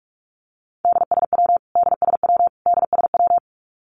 Morse code, heard as a keyed tone, about 44 wpm, 700 Hz. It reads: DSW DSW DSW
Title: Abbreviations Repeated 3x - 45wpm